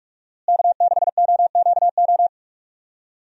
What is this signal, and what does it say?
Morse code, heard as a keyed tone, about 46 wpm, 700 Hz. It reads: K6OQO
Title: USA Call Signs - 5 Characters - Repeated 3x - 45wpm